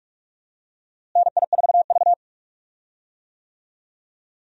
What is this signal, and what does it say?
Morse code, heard as a keyed tone, about 46 wpm, 700 Hz. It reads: NI4V
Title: USA Call Signs - 4 Characters - Repeated 2x - 45wpm